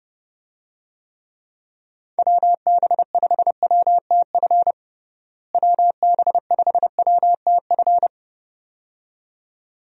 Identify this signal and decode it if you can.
Morse code, heard as a keyed tone, about 32 wpm, 700 Hz. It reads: WB5WTF WB5WTF
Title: USA Call Signs - 6 Characters - Repeated 2x - 30wpm